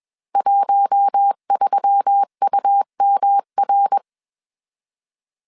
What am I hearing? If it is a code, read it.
Morse code, heard as a keyed tone, about 21 wpm, 800 Hz. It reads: 13UMR
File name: CW / DET / 1 6 1500C S B 20wpm 2048